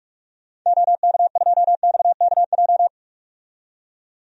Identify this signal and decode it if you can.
Morse code, heard as a keyed tone, about 46 wpm, 700 Hz. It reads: OK2XKJ
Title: DXCC-100 Call Signs - 6 Characters - 45wpm